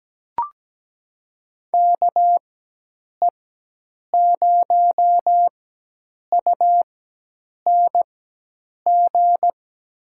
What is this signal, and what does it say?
Morse code, heard as a keyed tone, about 17 wpm, spaced as for 8 wpm, 700 Hz. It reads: KE0UNG
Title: USA Call Signs - 6 Characters - 4x Character Spacing - 17wpm